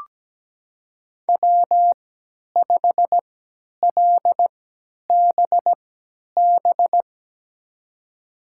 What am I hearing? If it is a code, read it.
Morse code, heard as a keyed tone, about 17 wpm, spaced as for 10 wpm, 700 Hz. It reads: W5LBB
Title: USA Call Signs - 5 Characters - 3x Character Spacing - 17wpm